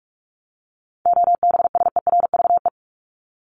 Morse code, heard as a keyed tone, about 44 wpm, 700 Hz. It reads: OBSERVE
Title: ED Suffix - Top 100 Words with Base Words 45wpm